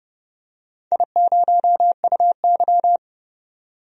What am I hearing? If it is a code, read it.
Morse code, heard as a keyed tone, about 32 wpm, 700 Hz. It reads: I0UY
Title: DXCC-100 Call Signs - 4 Characters - 30wpm